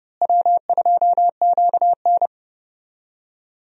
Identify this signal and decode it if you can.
Morse code, heard as a keyed tone, about 30 wpm, 700 Hz. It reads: W2QN
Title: USA Call Signs - 4 Characters - Repeated 3x - 30wpm